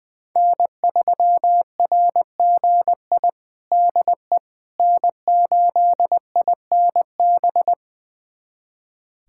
Morse code, heard as a keyed tone, about 21 wpm, 700 Hz. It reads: N3RGI DE N8INB
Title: USA Callsign DE Callsign - 5 Characters - 20wpm